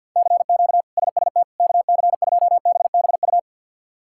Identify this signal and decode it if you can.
Morse code, heard as a keyed tone, about 52 wpm, 690 Hz. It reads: CQ SST KC2BBU
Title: K1USN SST QSO Practice - 50wpm